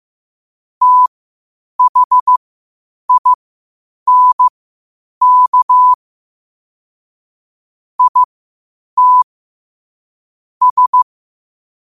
Morse code, heard as a keyed tone, about 15 wpm, spaced as for 8 wpm, 1000 Hz. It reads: THINK IT S
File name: 8wpm